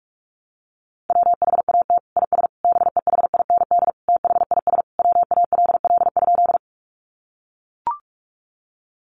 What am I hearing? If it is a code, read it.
Morse code, heard as a keyed tone, about 44 wpm, 700 Hz. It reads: WHAT IS BEHIND THIS WALL?
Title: Sentences from Top 1100 Words 45wpm